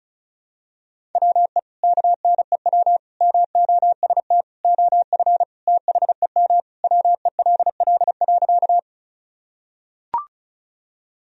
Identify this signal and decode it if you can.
Morse code, heard as a keyed tone, about 34 wpm, 700 Hz. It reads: WE KNEW MOST OF THEM WELL.
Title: Sentences from Top 1000 Words - Review Part 1 - 35wpm